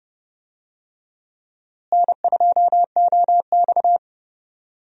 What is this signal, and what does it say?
Morse code, heard as a keyed tone, about 30 wpm, 700 Hz. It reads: N2OX
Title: USA Call Signs - 4 Characters - Repeated 3x - 30wpm